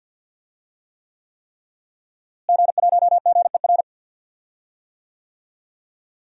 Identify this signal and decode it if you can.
Morse code, heard as a keyed tone, about 49 wpm, 700 Hz. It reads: G1GER
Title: DXCC-100 Call Signs - 5 Characters - Repeated 3x - 50wpm